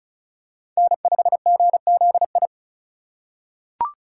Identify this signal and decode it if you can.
Morse code, heard as a keyed tone, about 36 wpm, 700 Hz. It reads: N5GZI
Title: USA Call Signs - 4x Character Spacing - 35wpm